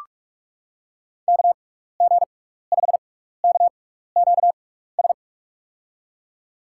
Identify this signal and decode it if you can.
Morse code, heard as a keyed tone, about 46 wpm, spaced as for 16 wpm, 700 Hz. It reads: KG5KQS
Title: USA Call Signs - 6x Character Spacing - 45wpm